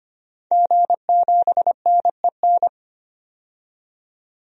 Morse code, heard as a keyed tone, about 25 wpm, 700 Hz. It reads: G7NEN
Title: DXCC-100 Call Signs - 5 Characters - 25wpm